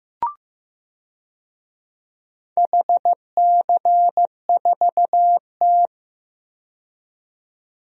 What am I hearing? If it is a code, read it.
Morse code, heard as a keyed tone, about 15 wpm, 700 Hz. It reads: HC4T
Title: DXCC-100 Call Signs - 4 Characters - 15wpm